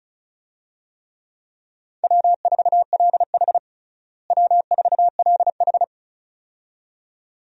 Morse code, heard as a keyed tone, about 36 wpm, 700 Hz. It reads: W4LH W4LH
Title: USA Call Signs - 4 Characters - Repeated 2x - 35wpm